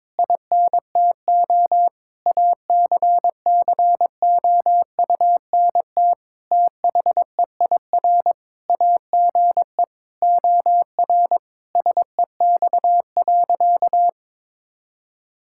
Morse code, heard as a keyed tone, about 21 wpm, 700 Hz. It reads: INTO ACCOUNT THEIR AGE OR SEX.
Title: Sentences from Top 1600 Words 22wpm